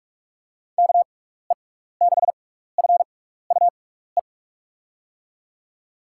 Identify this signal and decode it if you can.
Morse code, heard as a keyed tone, about 46 wpm, spaced as for 16 wpm, 700 Hz. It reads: KE6FUE
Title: USA Call Signs - 6 Characters - 6x Character Spacing - 45wpm